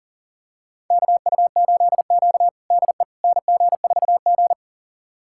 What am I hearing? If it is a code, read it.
Morse code, heard as a keyed tone, about 40 wpm, 700 Hz. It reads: KU8Q DE NG4G
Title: USA Callsign DE Callsign - 4 Characters - 40wpm